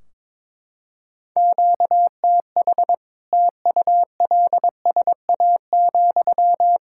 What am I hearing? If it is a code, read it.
Morse code, heard as a keyed tone, about 22 wpm, 700 Hz. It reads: QTH TULSA,
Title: Practice QSO - Exchange 1 - Beginning - 22wpm